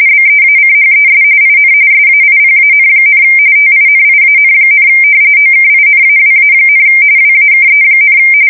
RTTY, about 45.45 baud. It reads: BEACHES IN SAIPAN, TINIAN, OKINAWA, WAKE,
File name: fsk170 / battleshipnj